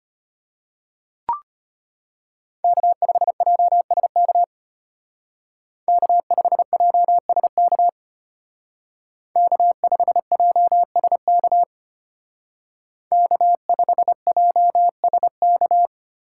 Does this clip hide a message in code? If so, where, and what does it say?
Yes: Morse code from 2.6 to 15.9 s, heard as a keyed tone, about 31 wpm, 700 Hz. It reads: K5JSK K5JSK K5JSK K5JSK